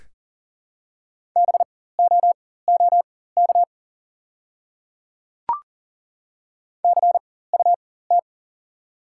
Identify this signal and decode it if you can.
Morse code, heard as a keyed tone, about 41 wpm, spaced as for 19 wpm, 690 Hz. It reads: BOOK CUT
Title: 4x Character Spacing - Top 500 Words - 40wpm